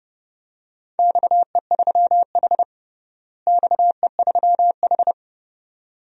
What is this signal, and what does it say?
Morse code, heard as a keyed tone, about 32 wpm, 700 Hz. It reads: XE3H XE3H
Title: DXCC-100 Call Signs - 4 Characters - Repeated 2x - 30wpm